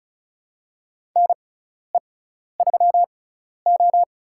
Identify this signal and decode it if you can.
Morse code, heard as a keyed tone, about 36 wpm, spaced as for 12 wpm, 700 Hz. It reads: NE3O
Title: USA Call Signs - 6x Character Spacing - 35wpm